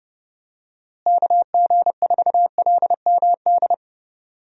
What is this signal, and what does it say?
Morse code, heard as a keyed tone, about 30 wpm, 700 Hz. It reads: KG4LMD
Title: USA Call Signs - 6 Characters - Repeated 2x - 30wpm